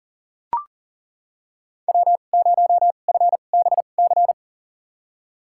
Morse code, heard as a keyed tone, about 40 wpm, spaced as for 29 wpm, 700 Hz. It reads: W0FBC
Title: USA Call Signs - 2x Character Spacing - 40wpm